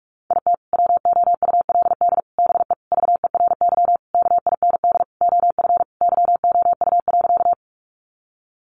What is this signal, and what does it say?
Morse code, heard as a keyed tone, about 44 wpm, 700 Hz. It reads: IT WOULD BE VERY KIND OF YOU.
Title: Sentences from Top 200 Words 45wpm